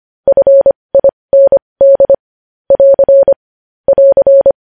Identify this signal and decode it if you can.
Morse code, heard as a keyed tone, about 25 wpm, spaced as for 20 wpm, 550 Hz. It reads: FIND <AR> <AR>
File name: podcast / News headlines-1x-25x20